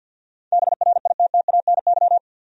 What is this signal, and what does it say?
Morse code, heard as a keyed tone, about 48 wpm, 700 Hz. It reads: BRITTANY
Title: USA Common Names - 50wpm